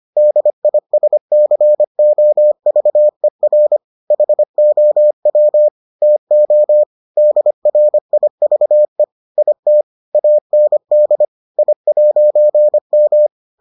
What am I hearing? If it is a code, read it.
Morse code, heard as a keyed tone, about 25 wpm, 600 Hz. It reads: DISCOVER HOW TO DRIVE IT AND I'M